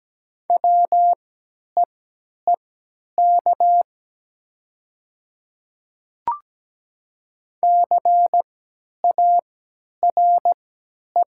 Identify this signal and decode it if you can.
Morse code, heard as a keyed tone, about 17 wpm, spaced as for 10 wpm, 700 Hz. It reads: WEEK CARE
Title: 3x Character Spacing - Top 500 Words - 17wpm